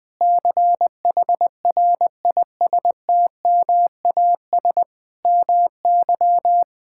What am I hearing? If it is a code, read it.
Morse code, heard as a keyed tone, about 20 wpm, 700 Hz. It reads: CHRISTMAS MY
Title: Sets of 9 Words - 20wpm